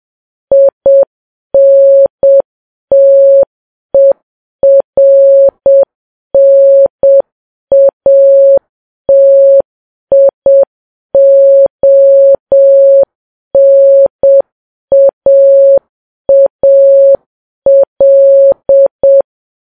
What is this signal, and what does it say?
Morse code, heard as a keyed tone, about 7 wpm, 550 Hz. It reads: INTERNATIONAAL
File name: Files Web tls / Morse on9fv / morse7kps